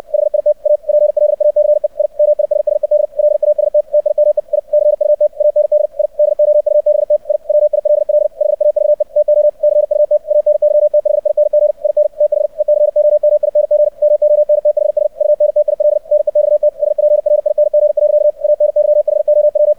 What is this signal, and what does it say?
Morse code, heard as a keyed tone, about 64 wpm, 600 Hz. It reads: HIT A PLACE I MISSED LAST TIME I CUT AND I DOUBT I WILL HAVE TO CUT ANYTHING IN THE COMING MONTHS UNTIL NEXT SPRING. ANYHOW
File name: archived / CRSnet